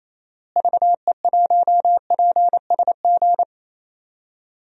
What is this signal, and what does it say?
Morse code, heard as a keyed tone, about 28 wpm, 700 Hz. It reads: VE1PSG
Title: DXCC-100 Call Signs - 6 Characters - Repeated 3x - 28wpm